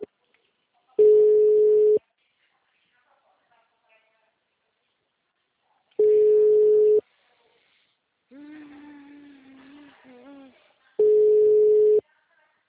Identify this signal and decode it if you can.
Ringback tone (Europe, 425 Hz), with speech.
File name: out- -2031- - - .54008